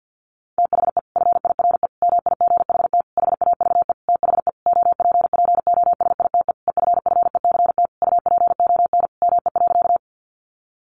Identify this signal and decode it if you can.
Morse code, heard as a keyed tone, about 48 wpm, 700 Hz. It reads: THE FIRE MIGHT HAVE THE OPPOSITE EFFECT UPON ME.
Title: Sentences from Top 1300 Words 50wpm